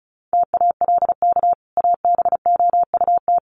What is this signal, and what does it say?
Morse code, heard as a keyed tone, about 34 wpm, 700 Hz. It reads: TALK ABOUT
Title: Sets of 2 Words 35wpm